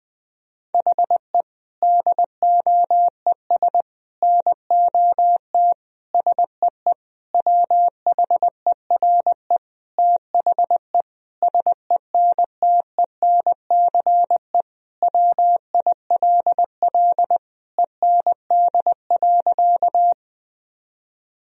Morse code, heard as a keyed tone, about 20 wpm, 700 Hz. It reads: HE DOES NOT SEE WHERE THE SENTENCE WILL END.
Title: Sentences from Top 1000 Words - Review Part 1 - 20wpm